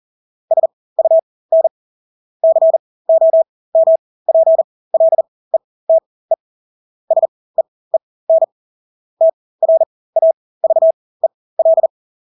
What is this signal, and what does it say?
Morse code, heard as a keyed tone, about 40 wpm, spaced as for 20 wpm, 650 Hz. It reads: SUN COMPLETE SEEN TRAVEL
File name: Engelske ord 40-20 wpm